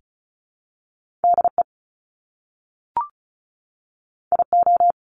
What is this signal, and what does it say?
Morse code, heard as a keyed tone, about 36 wpm, 700 Hz. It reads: DE IO
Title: Top Two Letter Combinations in Words 35wpm